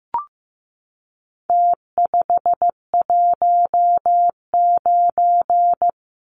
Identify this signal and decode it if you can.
Morse code, heard as a keyed tone, about 15 wpm, 700 Hz. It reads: T519
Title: Numbers - Contest Style - 4-Digit 15wpm